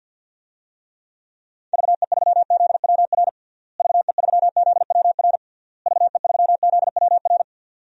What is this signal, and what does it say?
Morse code, heard as a keyed tone, about 51 wpm, 700 Hz. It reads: VE3ZWR VE3ZWR VE3ZWR
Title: DXCC-100 Call Signs - 6 Characters - Repeated 3x - 50wpm